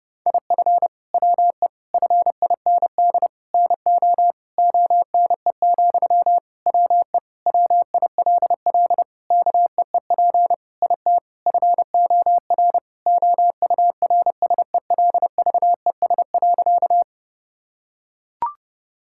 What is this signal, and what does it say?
Morse code, heard as a keyed tone, about 30 wpm, 700 Hz. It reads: IF WE FIND NO ONE, WE WILL KEEP IT FOR OURSELVES.
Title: Sentences from Top 1600 Words 30wpm